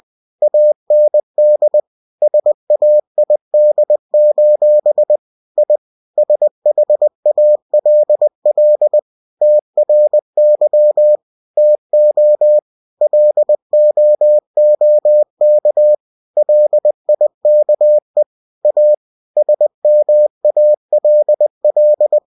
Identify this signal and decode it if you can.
Morse code, heard as a keyed tone, about 20 wpm, 600 Hz. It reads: AND SAID: I SHALL TRY TO LOOK LIKE A SMALL
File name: cw-books / winnie-the-pooh / winnie-20wpm-page-0006